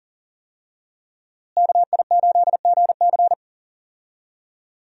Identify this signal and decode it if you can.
Morse code, heard as a keyed tone, about 37 wpm, 700 Hz. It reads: KI8GC
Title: USA Call Signs - 5 Characters - Repeated 3x - 40wpm